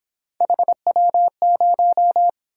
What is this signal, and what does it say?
Morse code, heard as a keyed tone, about 26 wpm, 700 Hz. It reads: HW0
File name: Speed-Racing - Three Letter-Number Combinations - 22wpm